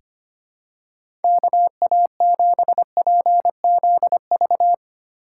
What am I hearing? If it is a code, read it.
Morse code, heard as a keyed tone, about 25 wpm, 700 Hz. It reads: KA7PZV
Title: USA Call Signs - 6 Characters - 3x Character Spacing - 25wpm